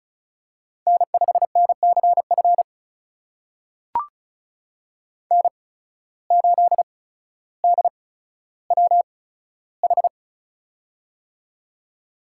Morse code, heard as a keyed tone, about 36 wpm, spaced as for 15 wpm, 700 Hz. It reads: N5NCF N8DWH
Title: USA Call Signs - 5 Characters - 8x Character Spacing - 35wpm